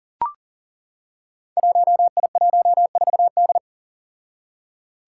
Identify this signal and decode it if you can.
Morse code, heard as a keyed tone, about 40 wpm, 700 Hz. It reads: 1I14D
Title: Five Challenging Letter-Number Combinations 40wpm